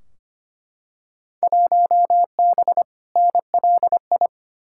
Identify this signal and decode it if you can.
Morse code, heard as a keyed tone, about 25 wpm, 700 Hz. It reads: 1B NLI
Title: Field Day Exchanges - 25wpm